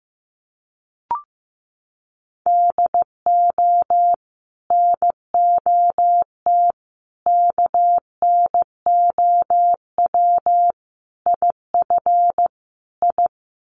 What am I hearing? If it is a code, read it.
Morse code, heard as a keyed tone, about 15 wpm, 700 Hz. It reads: DO NOT KNOW IF I
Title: Sets of 5 Words 15wpm